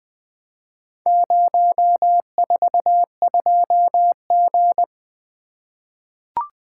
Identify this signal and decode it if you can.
Morse code, heard as a keyed tone, about 20 wpm, 700 Hz. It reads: 042G
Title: Four Letter-Number Combinations - 20wpm